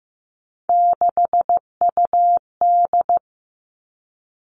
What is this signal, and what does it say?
Morse code, heard as a keyed tone, about 15 wpm, 700 Hz. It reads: 6UD
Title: All Cut Numbers - 3-Digit 15wpm